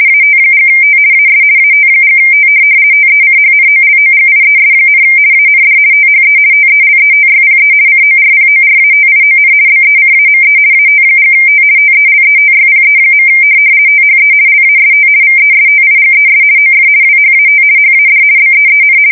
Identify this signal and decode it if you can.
RTTY, about 50 baud. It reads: AT&T AS AT&T TECHNOLOGIES, AND THE TELETYPE CORPORATION BECAME AT&T TELETYPE. THE LAST VESTIGES OF WHAT HAD BEEN
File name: fsk170-50B / teletype-corp